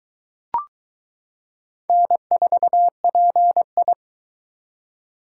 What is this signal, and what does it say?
Morse code, heard as a keyed tone, about 22 wpm, 700 Hz. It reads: N4PI